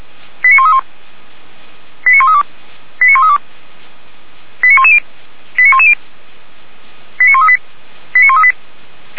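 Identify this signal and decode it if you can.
Five-tone selective call (ZVEI), heard as quick runs of five tones, tones of about 70 ms. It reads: 89131 89132 89132 89110 89110 89138 89138
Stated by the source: ZVEI five-tone selective call received by SDR, Feuerwehr Orlamünde fire brigade.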